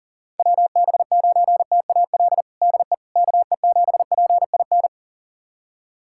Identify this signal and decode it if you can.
Morse code, heard as a keyed tone, about 37 wpm, 700 Hz. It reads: WB9TAL DE KE7PIN
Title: USA Callsign DE Callsign - 6 Characters - 40wpm